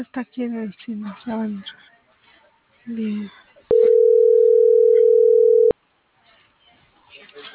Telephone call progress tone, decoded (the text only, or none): ringback tone